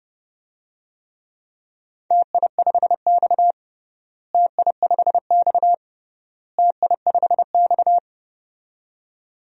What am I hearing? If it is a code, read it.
Morse code, heard as a keyed tone, about 30 wpm, 700 Hz. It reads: TI5X TI5X TI5X
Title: DXCC-100 Call Signs - 4 Characters - Repeated 3x - 30wpm